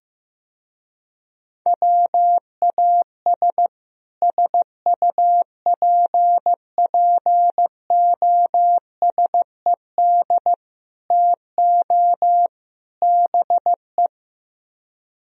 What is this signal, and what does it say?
Morse code, heard as a keyed tone, about 15 wpm, 700 Hz. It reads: WAS SUPPOSED TO BE
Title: Speed-Racing - Sets of 4 Words 15wpm